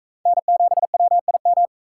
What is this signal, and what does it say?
Morse code, heard as a keyed tone, about 40 wpm, 700 Hz. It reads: N7WIM